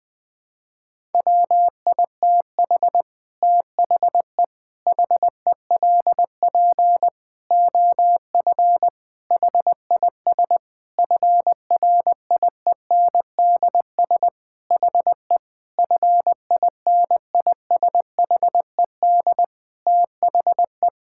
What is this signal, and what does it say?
Morse code, heard as a keyed tone, about 20 wpm, 700 Hz. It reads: WITH THE HELP OF HIS FRIENDS HE FINISHED THE
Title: Sentences with 4-Word Phrases - 20wpm